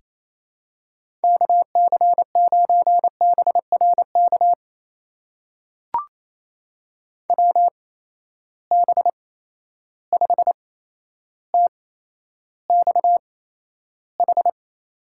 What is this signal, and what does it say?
Morse code, heard as a keyed tone, about 28 wpm, spaced as for 12 wpm, 700 Hz. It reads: KC9BRK WB5TXH
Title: USA Call Signs - 6 Characters - 8x Character Spacing - 28wpm